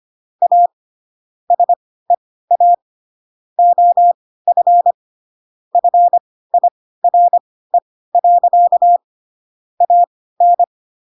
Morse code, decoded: A SEA OF FIRE. AN

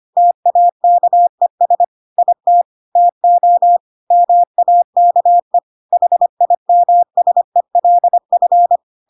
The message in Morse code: TAKES IT TO MAKE HIMSELF

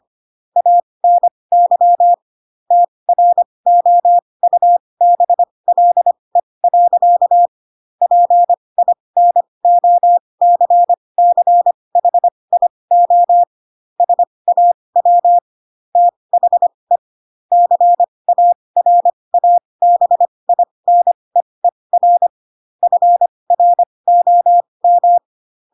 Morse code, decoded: ANY TROUBLE. PINOCCHIO SAW THE CARABINEER FROM

